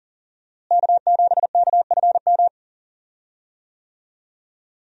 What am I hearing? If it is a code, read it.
Morse code, heard as a keyed tone, about 40 wpm, 700 Hz. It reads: K7KFM
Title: USA Call Signs - 5 Characters - Repeated 2x - 40wpm